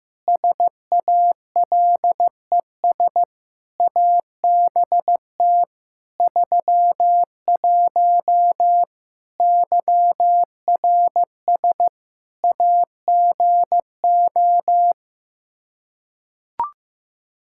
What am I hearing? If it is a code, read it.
Morse code, heard as a keyed tone, about 15 wpm, 700 Hz. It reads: SALES ABT 31 YRS AGO
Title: Practice QSO - Exchange 3 - More Info - 15wpm